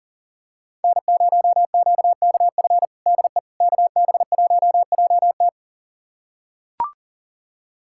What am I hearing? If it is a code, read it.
Morse code, heard as a keyed tone, about 40 wpm, 700 Hz. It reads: N0QKF DE KB1JT
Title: USA Callsign DE Callsign - 5 Characters - 40wpm